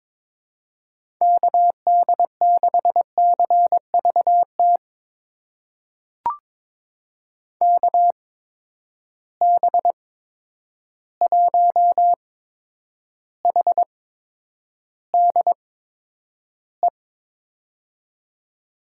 Morse code, heard as a keyed tone, about 22 wpm, spaced as for 9 wpm, 700 Hz. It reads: KD6CVT KB1HDE
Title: USA Call Signs - 6 Characters - 8x Character Spacing - 22wpm